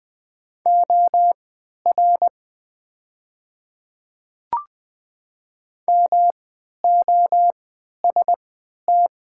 Morse code, heard as a keyed tone, about 20 wpm, spaced as for 11 wpm, 700 Hz. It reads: OR MOST
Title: 3x Character Spacing - Top 100 Words - 20wpm